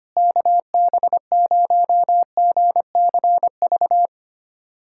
Morse code, heard as a keyed tone, about 25 wpm, 700 Hz. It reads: KB0GCV
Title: USA Callsign DE Callsign - 6 Characters - 25wpm